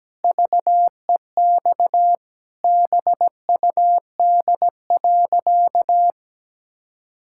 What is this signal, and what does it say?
Morse code, heard as a keyed tone, about 17 wpm, 700 Hz. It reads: VEX BUD.
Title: Pangram Challenge v2 - 17wpm